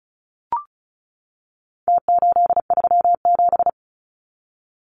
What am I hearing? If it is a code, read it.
Morse code, heard as a keyed tone, about 34 wpm, 700 Hz. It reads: T837
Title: Numbers - Contest Style - 4-Digit 35wpm